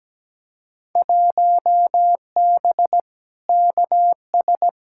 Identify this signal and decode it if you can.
Morse code, heard as a keyed tone, about 17 wpm, 700 Hz. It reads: 1B KS
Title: Field Day Exchanges Encore - 17wpm